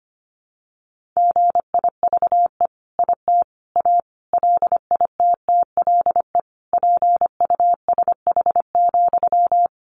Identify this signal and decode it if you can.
Morse code, heard as a keyed tone, about 25 wpm, 700 Hz. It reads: GIVE IT A LITTLE PUSH,
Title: Sentences from Top 1500 Words 25wpm